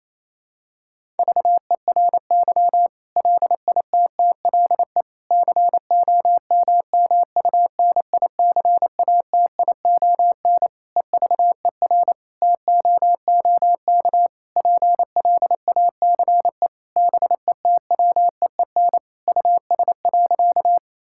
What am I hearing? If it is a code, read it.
Morse code, heard as a keyed tone, about 28 wpm, 700 Hz. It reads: VERY LITTLE COMMUNICATION EVER TOOK PLACE BETWEEN US.
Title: Sentences from Top 1000 Words - Review Part 6 - 28wpm